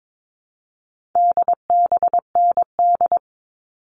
Morse code, heard as a keyed tone, about 22 wpm, 700 Hz. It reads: DBND
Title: All Cut Numbers - 4-Digit 22wpm